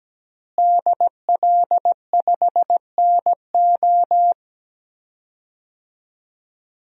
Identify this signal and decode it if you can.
Morse code, heard as a keyed tone, about 17 wpm, 700 Hz. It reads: DL5NO
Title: DXCC-100 Call Signs - 5 Characters - Repeated 3x - 17wpm